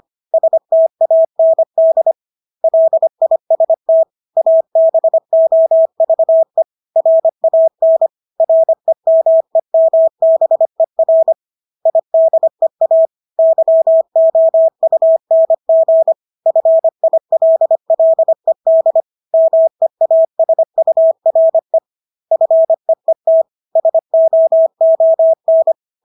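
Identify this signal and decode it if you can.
Morse code, heard as a keyed tone, about 25 wpm, 650 Hz. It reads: STAND LIST ABOVE RAN REMEMBER IDEA YOUNG FILLED MEASURE FEET SOON
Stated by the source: Morse practice of English words at 25 wpm from CW med Gnister.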